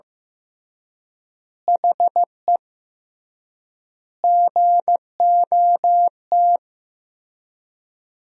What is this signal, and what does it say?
Morse code, heard as a keyed tone, about 15 wpm, 700 Hz. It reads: HE GOT
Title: 3x Word-Spacing - Sentences from Top 300 Words - 15wpm